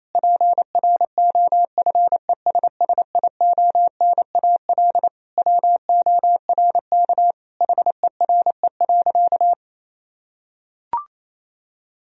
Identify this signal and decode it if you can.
Morse code, heard as a keyed tone, about 28 wpm, 700 Hz. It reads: PROFESSIONAL WORK HERE.
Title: Sentences from Top 1000 Words - Review Part 4 - 28wpm